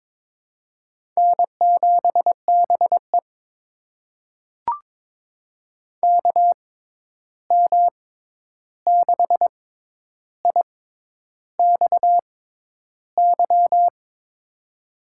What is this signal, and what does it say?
Morse code, heard as a keyed tone, about 22 wpm, spaced as for 10 wpm, 700 Hz. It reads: N7BE KM6IXY
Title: USA Call Signs - 6x Character Spacing - 22wpm